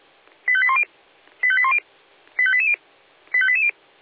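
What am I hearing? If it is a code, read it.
Five-tone selective call (ZVEI), heard as quick runs of five tones, tones of about 70 ms. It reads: 85810 85810 85880 85880
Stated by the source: A ZVEI five-tone selective call received by SDR.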